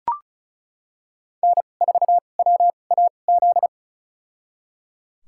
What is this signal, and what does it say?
Morse code, heard as a keyed tone, about 36 wpm, spaced as for 25 wpm, 700 Hz. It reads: N4WAZ
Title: USA Call Signs - 5 Characters - 2x Character Spacing - 35wpm